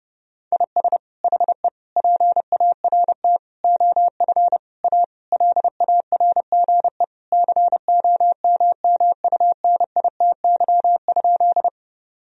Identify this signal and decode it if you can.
Morse code, heard as a keyed tone, about 30 wpm, 700 Hz. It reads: IS HE PART OF A LARGE COMMUNITY?
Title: Sentences from Top 1000 Words - Review Part 4 - 30wpm